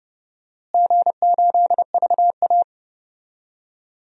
Morse code, heard as a keyed tone, about 30 wpm, 700 Hz. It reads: G8VA
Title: DXCC-100 Call Signs - 4 Characters - Repeated 3x - 30wpm